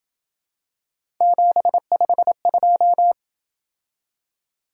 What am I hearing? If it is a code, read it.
Morse code, heard as a keyed tone, about 27 wpm, 700 Hz. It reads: 752